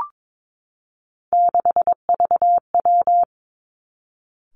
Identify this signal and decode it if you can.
Morse code, heard as a keyed tone, about 22 wpm, 700 Hz. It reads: 6VW
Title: Three Challenging Letter-Number Combinations 22wpm